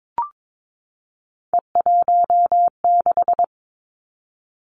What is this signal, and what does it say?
Morse code, heard as a keyed tone, about 22 wpm, 700 Hz. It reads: E16
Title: Three Challenging Letter-Number Combinations 22wpm